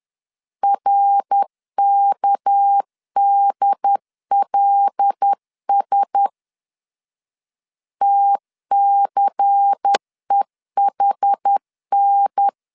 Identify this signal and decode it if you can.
Morse code, heard as a keyed tone, about 10 wpm, 800 Hz. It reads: RKDLS TCEHN